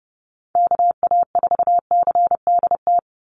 Morse code, heard as a keyed tone, about 30 wpm, 700 Hz. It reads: KA4CDT
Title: USA Call Signs 30wpm